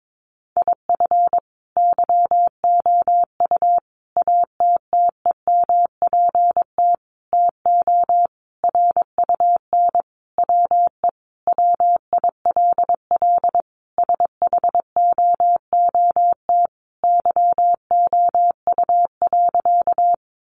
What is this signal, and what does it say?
Morse code, heard as a keyed tone, about 22 wpm, 700 Hz. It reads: IF YOU ATTEMPT TO RUN WE WILL SHOOT YOU.
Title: Sentences from Top 1200 Words 22wpm